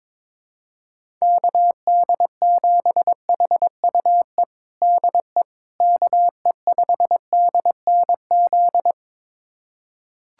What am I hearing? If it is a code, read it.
Morse code, heard as a keyed tone, about 21 wpm, 700 Hz. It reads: KD7HUE DE KE5DNZ
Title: USA Callsign DE Callsign - 6 Characters - 22wpm